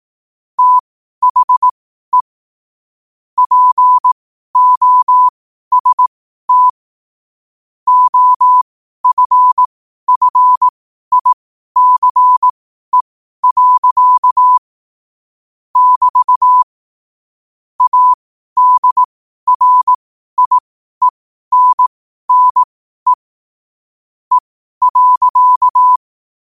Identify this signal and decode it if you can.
Morse code, heard as a keyed tone, about 18 wpm, spaced as for 12 wpm, 1000 Hz. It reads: THE POST OFFICE. <BT> ADRIENNE E.
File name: qotdfiles / 12wpm